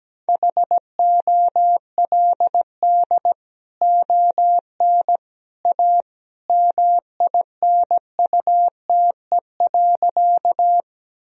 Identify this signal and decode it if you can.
Morse code, heard as a keyed tone, about 17 wpm, 700 Hz. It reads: HOLD ON A MINUTE.
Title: Sentences from Top 1000 Words - Review Part 2 - 17wpm